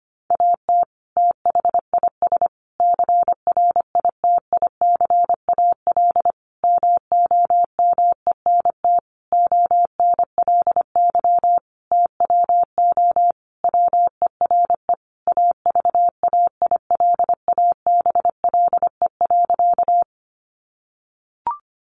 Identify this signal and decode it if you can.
Morse code, heard as a keyed tone, about 25 wpm, 700 Hz. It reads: AT THIS CRITICAL MOMENT ONLY TWO WERE AVAILABLE.
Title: Sentences from Top 1100 Words 25wpm